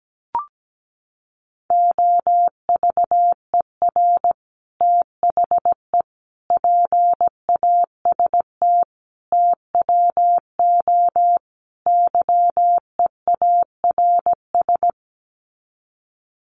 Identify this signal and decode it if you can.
Morse code, heard as a keyed tone, about 17 wpm, 700 Hz. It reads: OVER THE PAST TWO YEARS
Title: Sets of 5 Words Encore 17wpm